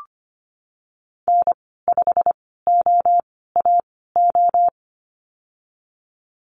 Morse code, heard as a keyed tone, about 25 wpm, spaced as for 16 wpm, 700 Hz. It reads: N5OAO
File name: Lesson 15g - Number 5 - Call Signs 25wpm